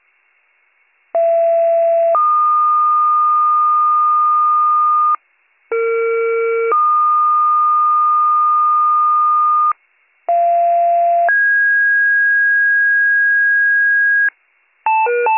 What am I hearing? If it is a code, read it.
Two-tone page, heard as a one-second tone, then a three-second tone, about 670 then 1220 Hz, about 460 then 1220 Hz, about 690 then 1650 Hz. It is heard from 1.1 to 14.3 s.